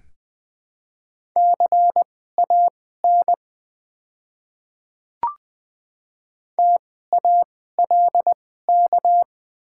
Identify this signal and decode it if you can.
Morse code, heard as a keyed tone, about 21 wpm, spaced as for 15 wpm, 700 Hz. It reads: CAN TALK
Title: 2x Character Spacing - Top 500 Words - 20wpm